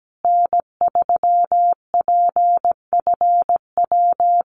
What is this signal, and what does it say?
Morse code, heard as a keyed tone, about 17 wpm, 700 Hz. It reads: N3PFW
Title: USA Call Signs Repeated 3x 17wpm